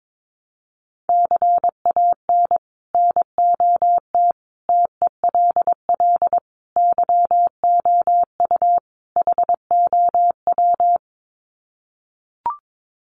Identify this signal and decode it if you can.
Morse code, heard as a keyed tone, about 22 wpm, 700 Hz. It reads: CAN NOT TELL YOU HOW
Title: Sets of 5 Words Encore2 22wpm